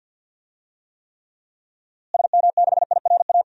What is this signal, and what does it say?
Morse code, heard as a keyed tone, about 51 wpm, 700 Hz. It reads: SM6IRA
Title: DXCC-100 Call Signs - 6 Characters - Repeated 3x - 50wpm